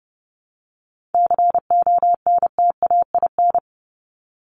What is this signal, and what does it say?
Morse code, heard as a keyed tone, about 30 wpm, 700 Hz. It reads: CONTAIN
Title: ING Suffix - Top 100 Words with Base Words 30wpm